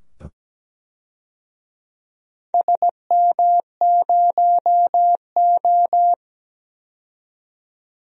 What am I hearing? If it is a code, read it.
Morse code, heard as a keyed tone, about 17 wpm, 700 Hz. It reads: SM0O
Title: DXCC-100 Call Signs - 4 Characters - 17wpm